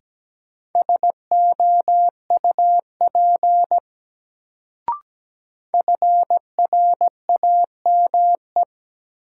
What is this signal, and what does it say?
Morse code, heard as a keyed tone, about 17 wpm, 700 Hz. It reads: SOUP FRAME
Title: Sentences from Top 1000 Words - Review Part 6 - New Word Drill - 17wpm